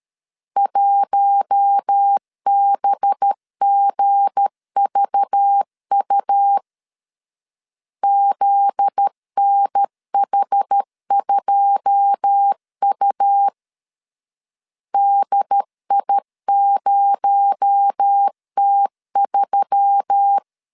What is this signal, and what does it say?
Morse code, heard as a keyed tone, about 13 wpm, 800 Hz. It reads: 1BGVU ZNH2U DI0T3
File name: CW / DET / 1 9 1500C S A 12wpm 3800